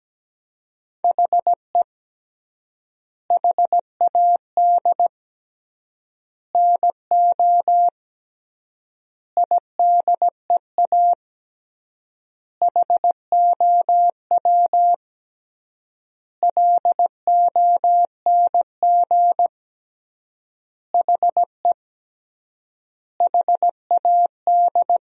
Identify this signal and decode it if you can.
Morse code, heard as a keyed tone, about 17 wpm, 700 Hz. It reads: HE HAD NO IDEA HOW LONG HE HAD
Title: 3x Word-Spacing - Sentences from Top 300 Words - 17wpm